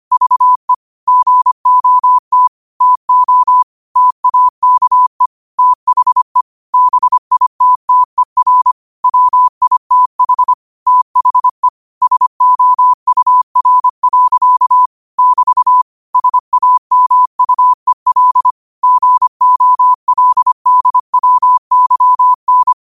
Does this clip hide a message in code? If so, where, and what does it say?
Yes: Morse code from 0.1 to 22.7 s, heard as a keyed tone, about 25 wpm, 1000 Hz. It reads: VE GOT TO TAKE THE BITTER WITH THE SOUR. <BT> SAMUEL GOLDWYN